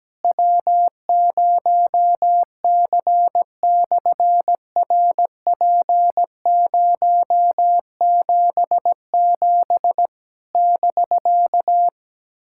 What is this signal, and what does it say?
Morse code, heard as a keyed tone, about 17 wpm, 700 Hz. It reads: W0C/RP077 <BK>
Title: SOTA QSO Practice 17wpm